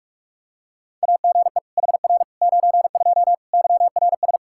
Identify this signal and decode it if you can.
Morse code, heard as a keyed tone, about 46 wpm, 700 Hz. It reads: AGE HR 92 YRS
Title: K1USN SST QSO Practice 45wpm